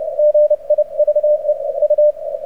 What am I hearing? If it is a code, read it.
Morse code, heard as a keyed tone, about 30 wpm, 590 Hz. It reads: GIVE U
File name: archived / CRSnet